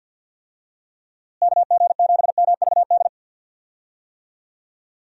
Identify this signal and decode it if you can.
Morse code, heard as a keyed tone, about 50 wpm, 700 Hz. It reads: KG7MVD
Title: USA Call Signs - 6 Characters - Repeated 2x - 50wpm